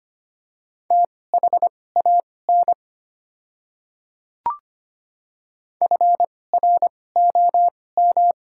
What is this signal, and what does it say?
Morse code, heard as a keyed tone, about 25 wpm, spaced as for 18 wpm, 690 Hz. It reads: THAN FROM
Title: 2x Character Spacing - Top 300 Words - 25wpm